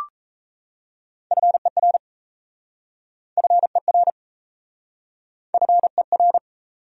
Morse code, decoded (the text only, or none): FER FER FER